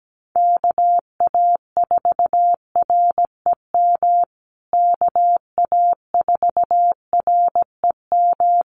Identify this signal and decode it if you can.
Morse code, heard as a keyed tone, about 17 wpm, 700 Hz. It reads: KA4REM KA4REM
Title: USA Call Signs Repeated 2x 17wpm